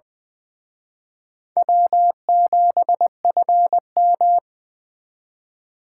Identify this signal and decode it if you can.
Morse code, heard as a keyed tone, about 23 wpm, 700 Hz. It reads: W7FM
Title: Speed Racing - USA 1x2 Call Signs 20wpm